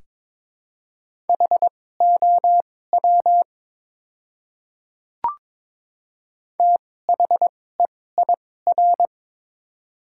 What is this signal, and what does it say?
Morse code, heard as a keyed tone, about 21 wpm, spaced as for 16 wpm, 700 Hz. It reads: HOW THEIR
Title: 2x Character Spacing - Top 100 Words - 22wpm